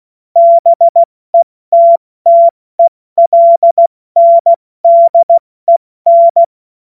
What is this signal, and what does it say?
Morse code, heard as a keyed tone, about 16 wpm, 680 Hz. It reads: BETTELNDEN
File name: Morsekurs / Witze / Witz 05. April 2022 70 F80